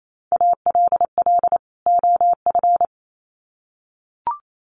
Morse code, heard as a keyed tone, about 28 wpm, 700 Hz. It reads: ALL OF
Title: Sets of 2 Words 28wpm